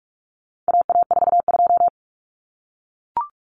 Morse code, heard as a keyed tone, about 44 wpm, 700 Hz. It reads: AA42
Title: Common Cut Numbers - 4-Digit 45wpm